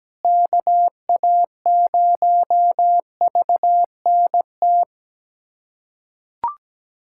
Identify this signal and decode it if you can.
Morse code, heard as a keyed tone, about 17 wpm, 700 Hz. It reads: KA0VNT
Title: USA Callsign DE Callsign - 6 Characters - 17wpm